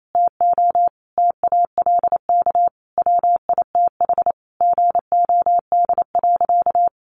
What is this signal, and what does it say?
Morse code, heard as a keyed tone, about 28 wpm, 700 Hz. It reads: TO TALK WITH GOD.
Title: A Sound Of Thunder 28wpm